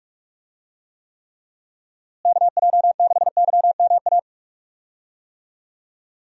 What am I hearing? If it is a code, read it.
Morse code, heard as a keyed tone, about 45 wpm, 700 Hz. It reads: KJ6YMA
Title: USA Call Signs - 6 Characters - Repeated 3x - 45wpm